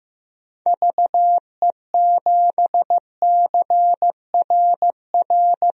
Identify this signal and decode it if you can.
Morse code, heard as a keyed tone, about 15 wpm, 700 Hz. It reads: VE7CRR
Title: DXCC-100 Call Signs - 6 Characters - 15wpm